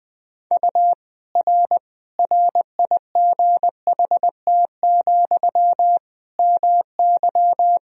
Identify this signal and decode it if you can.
Morse code, heard as a keyed tone, about 20 wpm, 700 Hz. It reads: U R RIGHT, MY
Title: Sentences from Top 400 Words and Abbreviations 20wpm